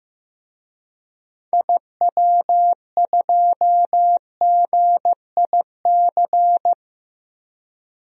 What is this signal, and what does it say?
Morse code, heard as a keyed tone, about 15 wpm, 700 Hz. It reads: IW2GIC
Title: DXCC-100 Call Signs - 6 Characters - Repeated 2x - 15wpm